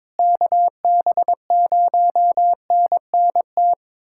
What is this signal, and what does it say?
Morse code, heard as a keyed tone, about 22 wpm, 700 Hz. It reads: KB0NNT
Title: USA Callsign DE Callsign - 6 Characters - 22wpm